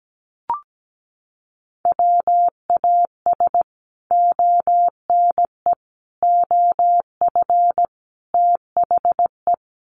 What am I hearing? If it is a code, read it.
Morse code, heard as a keyed tone, about 17 wpm, 700 Hz. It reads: WAS ONE OF THE
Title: Sets of 4 Words 17wpm